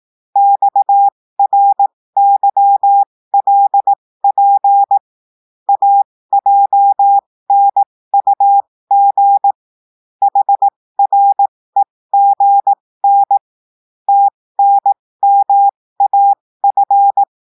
Morse code, heard as a keyed tone, about 18 wpm, spaced as for 15 wpm, 800 Hz. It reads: XRYLP AJNUG HREGN TNMAF
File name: cours-du-lundi files / cours-1580-texte-chiffre-15-mpm